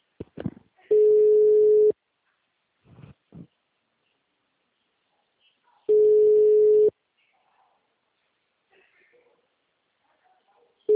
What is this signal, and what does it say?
Ringback tone (Europe, 425 Hz), with speech.